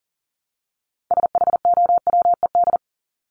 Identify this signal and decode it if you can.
Morse code, heard as a keyed tone, about 40 wpm, 700 Hz. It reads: SHOWED
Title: ED Suffix - Top 100 Words with Base Words 40wpm